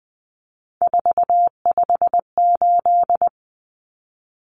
Morse code, heard as a keyed tone, about 20 wpm, 700 Hz. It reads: 458
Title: Numbers - 3-Digit 20wpm